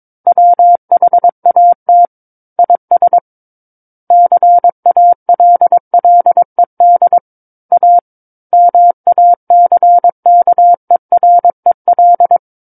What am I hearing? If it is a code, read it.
Morse code, heard as a keyed tone, about 22 wpm, 700 Hz. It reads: WHAT IS CALLED A MACKEREL
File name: Book One / 13-Chapter 13 - 22 WPM